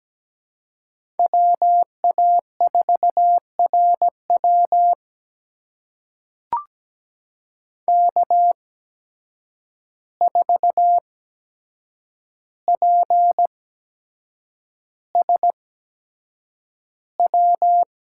Morse code, heard as a keyed tone, about 17 wpm, spaced as for 7 wpm, 700 Hz. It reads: WA4RW K4PSW
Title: USA Call Signs - 5 Characters - 8x Character Spacing - 17wpm